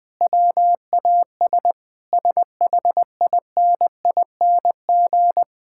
Morse code, heard as a keyed tone, about 20 wpm, 700 Hz. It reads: WAS SHINING
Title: Sentences with 4-Word Phrases - 20wpm